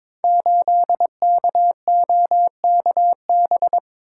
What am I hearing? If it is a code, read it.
Morse code, heard as a keyed tone, about 22 wpm, 700 Hz. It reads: 8KOKB
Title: Code Groups - 5 Letter:Numbers - 22wpm